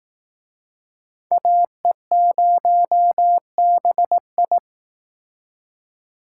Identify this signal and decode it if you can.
Morse code, heard as a keyed tone, about 18 wpm, 700 Hz. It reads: AE0BI
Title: Speed Racing - USA 2x2 Call Signs 15wpm